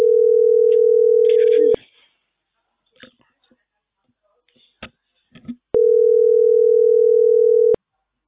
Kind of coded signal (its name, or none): telephone call progress tone